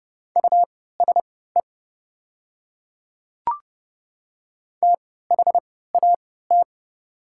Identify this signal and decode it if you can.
Morse code, heard as a keyed tone, about 30 wpm, spaced as for 17 wpm, 700 Hz. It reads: USE THAT
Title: 3x Character Spacing - Top 100 Words - 30wpm